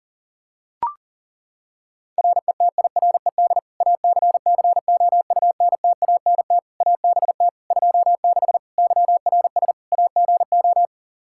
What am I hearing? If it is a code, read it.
Morse code, heard as a keyed tone, about 40 wpm, 700 Hz. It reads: RETIRED ACCOUNTANT ABT 26 YRS AGO
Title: Practice QSO - Exchange 3 - More Info - 40wpm